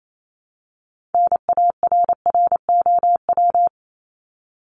Morse code, heard as a keyed tone, about 28 wpm, 700 Hz. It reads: NARROW
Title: Top 1501-1750 Words 28wpm